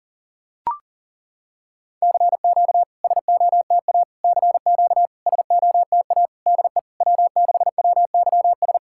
Morse code, heard as a keyed tone, about 40 wpm, 700 Hz. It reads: CQ SOTA CQ SOTA DE W6WYS
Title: SOTA QSO Practice 40wpm